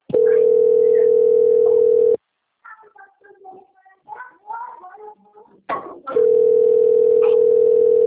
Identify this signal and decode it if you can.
Ringback tone (North America), 440 + 480 Hz, with speech.
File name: out- - - - - .4264